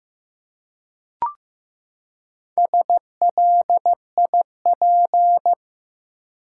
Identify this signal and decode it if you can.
Morse code, heard as a keyed tone, about 16 wpm, 700 Hz. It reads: SLIP
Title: Four Letter Words - 15wpm